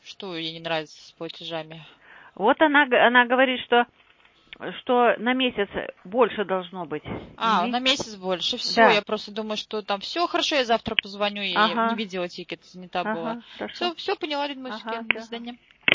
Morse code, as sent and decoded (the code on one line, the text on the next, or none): none